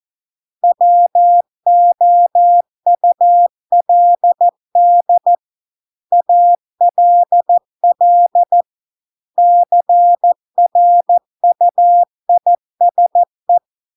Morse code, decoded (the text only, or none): WOULD ALL CRUISE